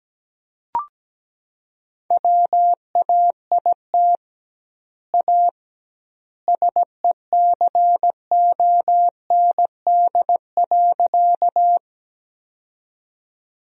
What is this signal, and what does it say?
Morse code, heard as a keyed tone, about 17 wpm, 700 Hz. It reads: WAIT A SECOND.
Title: 2x Word-Spacing - Sentences from Top 500 Words - 17wpm